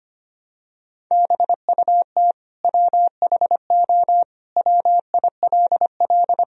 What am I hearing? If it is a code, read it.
Morse code, heard as a keyed tone, about 25 wpm, 700 Hz. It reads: BUT WHO WILL
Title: Sentences from Top 1000 Words - Review Part 1 - 25wpm